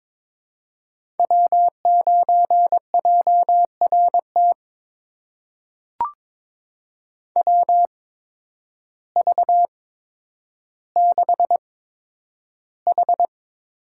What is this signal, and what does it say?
Morse code, heard as a keyed tone, about 23 wpm, spaced as for 10 wpm, 700 Hz. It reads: W9JRT WV6H
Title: USA Call Signs - 5 Characters - 8x Character Spacing - 22wpm